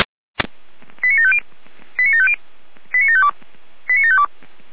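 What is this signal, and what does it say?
Five-tone selective call (ZVEI), heard as quick runs of five tones, tones of about 70 ms. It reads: 89750 89750 89752 89752
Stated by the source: ZVEI five-tone call received by SDR.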